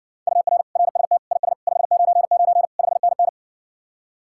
Morse code, heard as a keyed tone, about 67 wpm, 700 Hz. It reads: UR RST IS 599 5NN